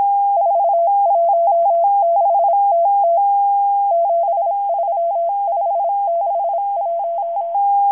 Morse code, heard as a keyed tone, about 26 wpm, 680 Hz. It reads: 416TT 73561
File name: HM02- -0437utc-7351khz